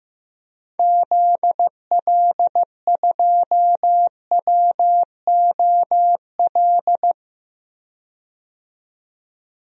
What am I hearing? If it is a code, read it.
Morse code, heard as a keyed tone, about 15 wpm, 700 Hz. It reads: ZL2WOL
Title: DXCC-100 Call Signs - 6 Characters - 15wpm